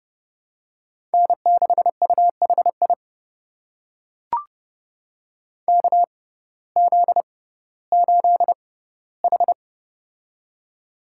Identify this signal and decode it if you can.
Morse code, heard as a keyed tone, about 30 wpm, spaced as for 17 wpm, 700 Hz. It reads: N6UHI KZ8H
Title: USA Call Signs - 6x Character Spacing - 30wpm